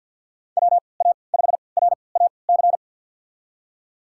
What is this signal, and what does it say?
Morse code, heard as a keyed tone, about 51 wpm, spaced as for 28 wpm, 690 Hz. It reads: WA5RAC
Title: USA Call Signs - 6 Characters - 3x Character Spacing - 50wpm